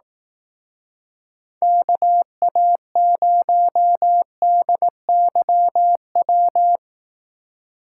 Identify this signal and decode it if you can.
Morse code, heard as a keyed tone, about 17 wpm, 700 Hz. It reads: KA0DYW
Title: Speed Racing - USA Call Signs 15wpm